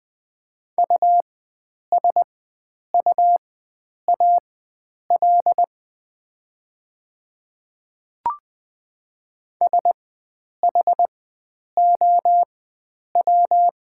Morse code, heard as a keyed tone, about 20 wpm, spaced as for 9 wpm, 700 Hz. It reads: USUAL SHOW
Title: 4x Character Spacing - Top 500 Words - 20wpm